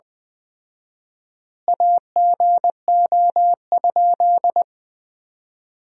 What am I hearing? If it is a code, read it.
Morse code, heard as a keyed tone, about 20 wpm, 700 Hz. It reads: AGO?
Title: 4x Word-Spacing - Sentences from Top 500 Words - 20wpm